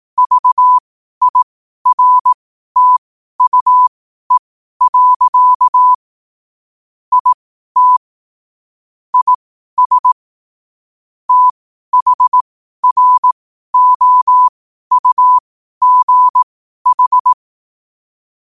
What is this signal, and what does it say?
Morse code, heard as a keyed tone, about 18 wpm, spaced as for 12 wpm, 1000 Hz. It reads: VIRTUE. IT IS THROUGH
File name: qotdfiles / 12wpm